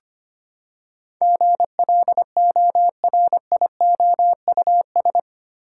Morse code, heard as a keyed tone, about 25 wpm, 700 Hz. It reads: GLORIOUS
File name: Eight Letter Words - 25wpm